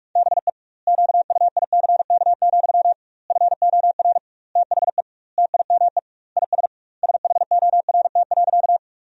Morse code, heard as a keyed tone, about 47 wpm, 700 Hz. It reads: BE QUICK, FOR THE TIME IS SHORT.
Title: 2x Word-Spacing - Sentences from Top 500 Words - 45wpm